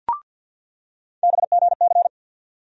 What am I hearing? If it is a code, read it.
Morse code, heard as a keyed tone, about 49 wpm, 700 Hz. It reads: BGC
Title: USA Call Sign Postfixes - 3 Letters 50wpm